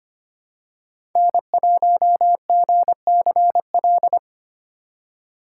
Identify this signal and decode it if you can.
Morse code, heard as a keyed tone, about 25 wpm, 700 Hz. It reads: N1GCL
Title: USA Call Signs - 5 Characters - 4x Character Spacing - 25wpm